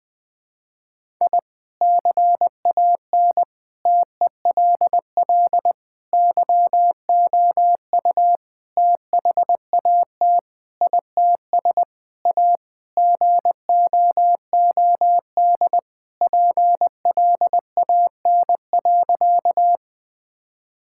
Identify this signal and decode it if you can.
Morse code, heard as a keyed tone, about 20 wpm, 700 Hz. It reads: I CAN TELL YOU THAT ITS A GOOD PLAN.
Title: Sentences with 5-Word Phrases - 20wpm